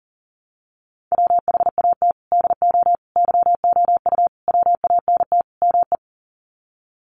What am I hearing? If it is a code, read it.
Morse code, heard as a keyed tone, about 40 wpm, 700 Hz. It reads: WHAT DO YOU WANT ME
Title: Sets of 5 Words Encore2 40wpm